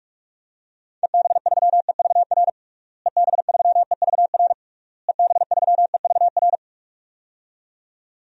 Morse code, heard as a keyed tone, about 41 wpm, 700 Hz. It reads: EB3EVR EB3EVR EB3EVR
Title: DXCC-100 Call Signs - 6 Characters - Repeated 3x - 45wpm